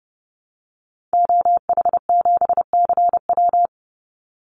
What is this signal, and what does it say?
Morse code, heard as a keyed tone, about 30 wpm, 700 Hz. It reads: OH7CW
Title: Call Signs Repeated 2x - 30wpm